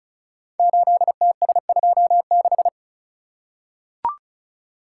Morse code, heard as a keyed tone, about 36 wpm, 700 Hz. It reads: 8TS26
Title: Code Groups - 5 Letter:Numbers - 35wpm